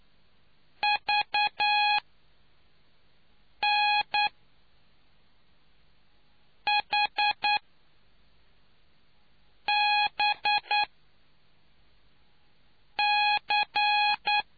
Morse code, heard as a keyed tone, about 9 wpm, spaced as for 4 wpm, 830 Hz. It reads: VNHBC